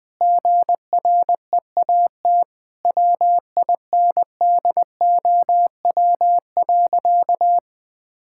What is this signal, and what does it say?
Morse code, heard as a keyed tone, about 21 wpm, 700 Hz. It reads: GREAT WINDOW.
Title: Sentences from Top 1600 Words 20wpm